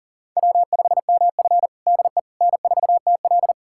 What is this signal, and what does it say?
Morse code, heard as a keyed tone, about 40 wpm, 700 Hz. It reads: W5MF DE N4TL
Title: USA Callsign DE Callsign - 4 Characters - 40wpm